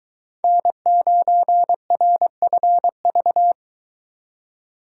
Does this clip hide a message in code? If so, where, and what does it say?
Yes: Morse code from 0.4 to 3.5 s, heard as a keyed tone, about 23 wpm, 700 Hz. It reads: N9RFV